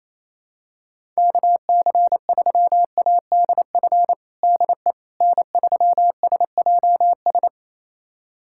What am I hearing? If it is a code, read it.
Morse code, heard as a keyed tone, about 28 wpm, 700 Hz. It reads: KC3ADF DE N3SJS
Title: USA Callsign DE Callsign 28wpm